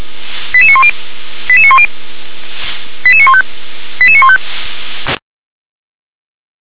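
Five-tone selective call (ZVEI), heard as quick runs of five tones, tones of about 70 ms. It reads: 88910 88910 88915 88915
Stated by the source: ZVEI five-tone call received on SDR.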